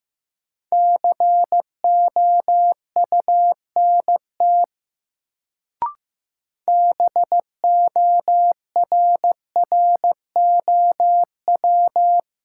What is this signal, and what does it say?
Morse code, heard as a keyed tone, about 15 wpm, 700 Hz. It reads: COUNT BORROW
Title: Sentences from Top 1500 Words - New Word Drill - 15wpm